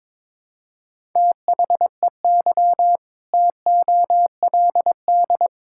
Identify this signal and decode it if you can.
Morse code, heard as a keyed tone, about 22 wpm, 700 Hz. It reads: THEY TOLD
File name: practice / 22wpm / Sentences with 2-Word Phrases - 22wpm